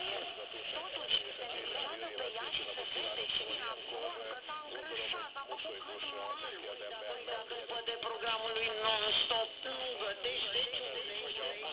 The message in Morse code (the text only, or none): none